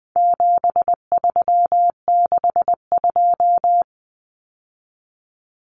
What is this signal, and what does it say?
Morse code, heard as a keyed tone, about 20 wpm, 700 Hz. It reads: 7362
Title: Numbers - 4-Digit 20wpm